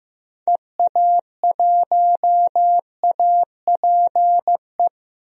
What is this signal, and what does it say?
Morse code, heard as a keyed tone, about 16 wpm, 700 Hz. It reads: EA1APE
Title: DXCC-100 Call Signs - 6 Characters - Repeated 2x - 15wpm